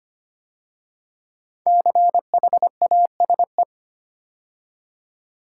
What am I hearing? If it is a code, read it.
Morse code, heard as a keyed tone, about 25 wpm, 700 Hz. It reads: CHASE
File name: 5 Letter Words - First 3 Letters Given - 25wpm